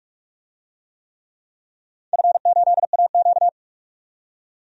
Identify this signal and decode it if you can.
Morse code, heard as a keyed tone, about 43 wpm, 700 Hz. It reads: F8AQ
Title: DXCC-100 Call Signs - 4 Characters - 45wpm